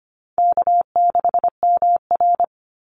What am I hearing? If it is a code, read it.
Morse code, heard as a keyed tone, about 25 wpm, 700 Hz. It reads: K6MR
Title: Easy Call Signs - 25wpm